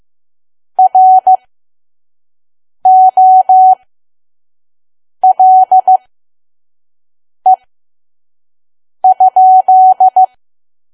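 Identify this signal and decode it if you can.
Morse code, heard as a keyed tone, about 15 wpm, spaced as for 5 wpm, 750 Hz. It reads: ROLE?
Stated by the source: Morse code practice at 5 WPM from KQ2P.COM.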